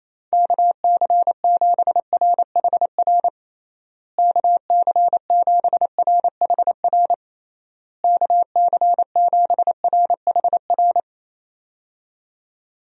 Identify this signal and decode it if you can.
Morse code, heard as a keyed tone, about 28 wpm, 700 Hz. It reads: KC7RHR KC7RHR KC7RHR
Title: USA Call Signs - 6 Characters - Repeated 3x - 28wpm